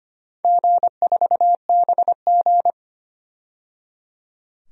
Morse code, heard as a keyed tone, about 25 wpm, 700 Hz. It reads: G4BG
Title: DXCC-100 Call Signs - 4 Characters - 25wpm